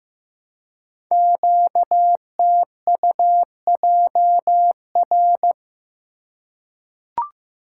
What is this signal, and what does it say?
Morse code, heard as a keyed tone, about 15 wpm, 700 Hz. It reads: QTUJR
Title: Code Groups - 5 Letters - 15wpm